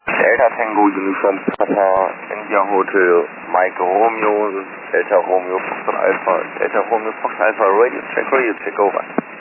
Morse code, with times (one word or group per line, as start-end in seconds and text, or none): none